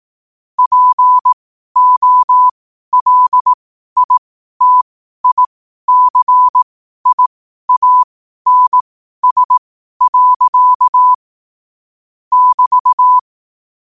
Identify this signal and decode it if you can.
Morse code, heard as a keyed tone, about 18 wpm, spaced as for 12 wpm, 1000 Hz. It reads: POLITICIANS. <BT>
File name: qotdfiles / 12wpm